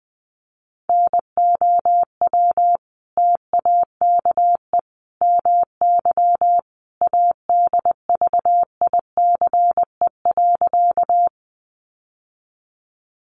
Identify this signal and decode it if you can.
Morse code, heard as a keyed tone, about 20 wpm, 700 Hz. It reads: NOW TAKE MY ADVICE.
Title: Sentences from Top 1100 Words 20wpm